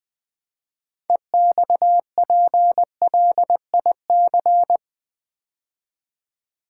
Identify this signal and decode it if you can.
Morse code, heard as a keyed tone, about 20 wpm, 700 Hz. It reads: EXPLIC
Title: Build A Word - 2 Letters At A Time - 8-Letter Words - 20wpm